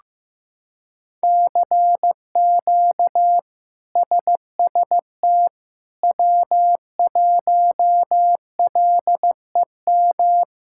Morse code, heard as a keyed tone, about 15 wpm, 700 Hz. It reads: CQ SST W1LEM
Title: K1USN SST QSO Practice - 15wpm